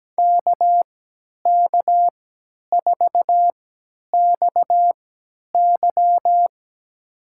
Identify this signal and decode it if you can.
Morse code, heard as a keyed tone, about 17 wpm, spaced as for 10 wpm, 700 Hz. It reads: KK4XY
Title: USA Call Signs - 5 Characters - 3x Character Spacing - 17wpm